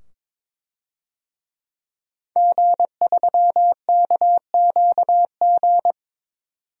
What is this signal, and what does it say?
Morse code, heard as a keyed tone, about 22 wpm, 700 Hz. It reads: G3KQG
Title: DXCC-100 Call Signs - 5 Characters - 22wpm